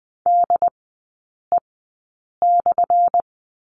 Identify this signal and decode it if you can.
Morse code, heard as a keyed tone, about 20 wpm, spaced as for 8 wpm, 700 Hz. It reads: DE/
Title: 184 Lesson 30e - Letter b - Three Characters 20wpm